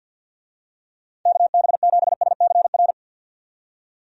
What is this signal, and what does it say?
Morse code, heard as a keyed tone, about 51 wpm, 700 Hz. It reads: KB7SCR
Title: USA Call Signs - 6 Characters - 50wpm